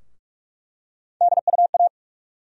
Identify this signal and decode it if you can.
Morse code, heard as a keyed tone, about 47 wpm, 700 Hz. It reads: DUA
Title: USA Call Sign Postfixes - 3 Letters 45wpm